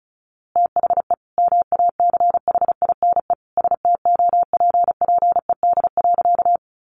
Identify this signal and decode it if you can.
Morse code, heard as a keyed tone, about 34 wpm, 700 Hz. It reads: THE MACHINE STOPPED.
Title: A Sound Of Thunder 35wpm